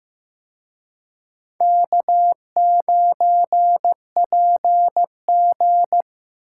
Morse code, heard as a keyed tone, about 18 wpm, 700 Hz. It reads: K9PG
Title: Speed Racing - USA 1x2 Call Signs 15wpm